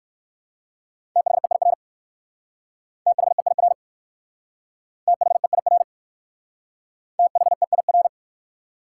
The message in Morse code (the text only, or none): THEIR THEIR THEIR THEIR